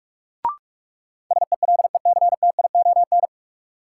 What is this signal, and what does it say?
Morse code, heard as a keyed tone, about 42 wpm, 700 Hz. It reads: SELECTION
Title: Sentences from Top 1400 Words - New Word Drill - 45wpm